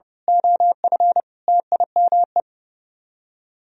Morse code, heard as a keyed tone, about 29 wpm, 700 Hz. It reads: OF TIME
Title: Speed-Racing - Sets of 5 Words 30wpm